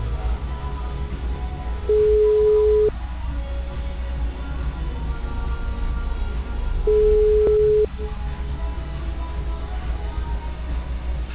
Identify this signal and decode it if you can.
Ringback tone (Europe, 425 Hz).